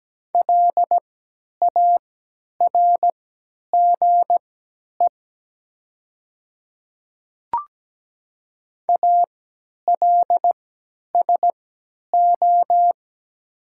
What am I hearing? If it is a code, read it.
Morse code, heard as a keyed tone, about 17 wpm, spaced as for 10 wpm, 700 Hz. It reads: LARGE ALSO
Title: 3x Character Spacing - Top 200 Words - 17wpm